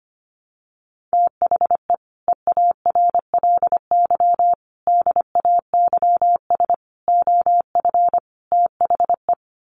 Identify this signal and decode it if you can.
Morse code, heard as a keyed tone, about 25 wpm, 700 Hz. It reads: THE EARLY DAYS OF THE
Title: Sets of 5 Words Encore 25wpm